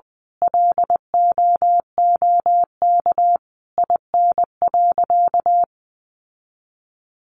Morse code, heard as a keyed tone, about 20 wpm, 700 Hz. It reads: LOOK IN.
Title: Sentences from Top 1300 Words 20wpm